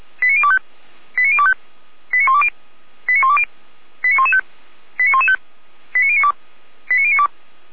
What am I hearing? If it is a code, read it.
Five-tone selective call (ZVEI), heard as quick runs of five tones, tones of about 70 ms. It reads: 89025 89025 89120 89120 89115 89115 89092 89092
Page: /455/SDR/ZVEI/